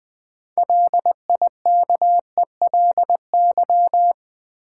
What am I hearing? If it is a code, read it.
Morse code, heard as a keyed tone, about 21 wpm, 700 Hz. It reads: LIKELY
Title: Sentences from Top 600 Words - New Word Drill - 20wpm